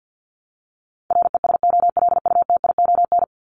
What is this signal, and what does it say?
Morse code, heard as a keyed tone, about 50 wpm, 700 Hz. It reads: RESOLUTION
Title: Top 1501-1750 Words 50wpm